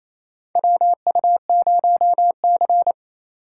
Morse code, heard as a keyed tone, about 28 wpm, 700 Hz. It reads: WU0C
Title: USA Call Signs - 4 Characters - 6x Character Spacing - 28wpm